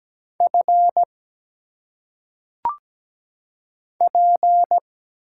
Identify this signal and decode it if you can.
Morse code, heard as a keyed tone, about 17 wpm, 700 Hz. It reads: F P
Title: Single Letters - Phonetic Pronunciation 17wpm